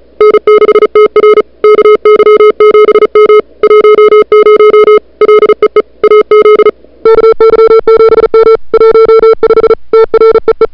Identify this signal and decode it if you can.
Morse code, heard as a keyed tone, about 34 wpm, 440 Hz. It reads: N6TR KY7M 10 LEE AZ KY7M 15 TREE
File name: CWSprint / February 2007 CW Sprint / 01-2007 CW Sprint Hour1